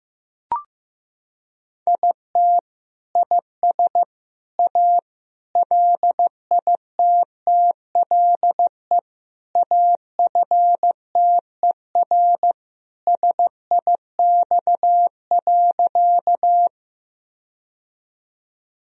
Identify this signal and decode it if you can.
Morse code, heard as a keyed tone, about 16 wpm, 700 Hz. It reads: IT IS A LITTLE AFTER SIX.
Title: Sentences from Top 1000 Words - Review Part 2 - 15wpm